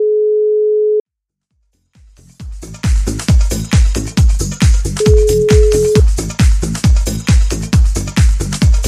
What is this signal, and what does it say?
Ringback tone (Europe, 425 Hz).